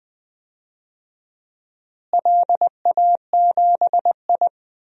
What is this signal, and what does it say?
Morse code, heard as a keyed tone, about 20 wpm, 700 Hz. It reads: LA7I
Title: DXCC-100 Call Signs - 4 Characters - Repeated 3x - 20wpm